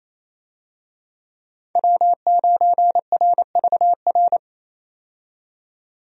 Morse code, heard as a keyed tone, about 28 wpm, 700 Hz. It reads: W9RVR
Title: USA Call Signs - 5 Characters - 28wpm